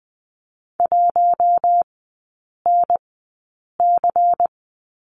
Morse code, heard as a keyed tone, about 20 wpm, spaced as for 8 wpm, 700 Hz. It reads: 1NC
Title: 093 Lesson 16e - Letter c - Three Characters 20wpm